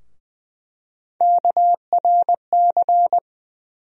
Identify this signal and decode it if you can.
Morse code, heard as a keyed tone, about 21 wpm, 700 Hz. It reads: KRC
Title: USA Call Sign Postfixes - 3 Letters 20wpm